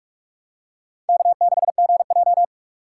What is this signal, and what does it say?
Morse code, heard as a keyed tone, about 46 wpm, 700 Hz. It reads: K6GJ
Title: USA Call Signs - 4 Characters - 2x Character Spacing - 45wpm